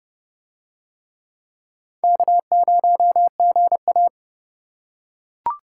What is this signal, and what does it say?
Morse code, heard as a keyed tone, about 29 wpm, 700 Hz. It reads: K0GA